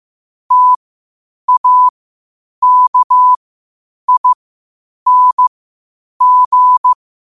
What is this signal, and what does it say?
Morse code, heard as a keyed tone, about 15 wpm, spaced as for 8 wpm, 1000 Hz. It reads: TAKING
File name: qotdfiles / 8wpm